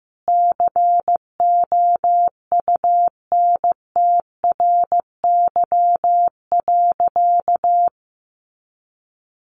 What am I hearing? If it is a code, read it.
Morse code, heard as a keyed tone, about 15 wpm, 700 Hz. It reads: COUNTRY.
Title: Sentences from Top 300 Words 15wpm